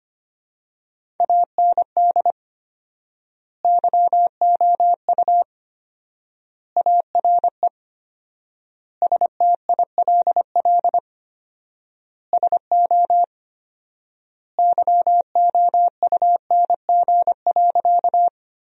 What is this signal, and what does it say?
Morse code, heard as a keyed tone, about 25 wpm, 700 Hz. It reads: AND YOU ARE STILL SO YOUNG.
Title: 4x Word-Spacing - Sentences from Top 400 Words - 25wpm